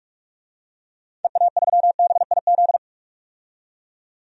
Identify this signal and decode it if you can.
Morse code, heard as a keyed tone, about 46 wpm, 700 Hz. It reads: EA3BIZ
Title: DXCC-100 Call Signs - 6 Characters - 45wpm